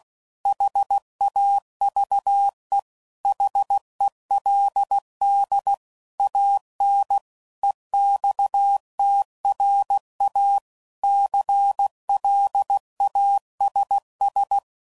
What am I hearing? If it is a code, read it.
Morse code, heard as a keyed tone, about 16 wpm, 800 Hz. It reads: HAVE HELD AN EXTRA CLASS
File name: CW Code Groups @15WPM / QSOs / kc7sac